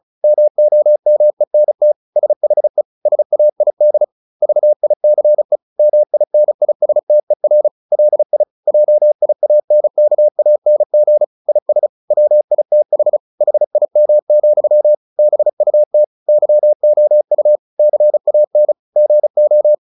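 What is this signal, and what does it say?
Morse code, heard as a keyed tone, about 34 wpm, 600 Hz. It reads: MOMENT SHE SAID VICE MINISTER LI JIANKANG IS WITH HIM, BUT YOU CAN GO
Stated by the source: Morse practice at 35 wpm from CW med Gnister, sending Ken Follett's novel Never in English.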